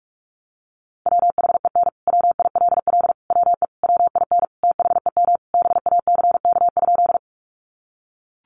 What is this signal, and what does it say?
Morse code, heard as a keyed tone, about 44 wpm, 700 Hz. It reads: WHEN WILL WE WIN THEM BACK?
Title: Sentences from Top 1500 Words 45wpm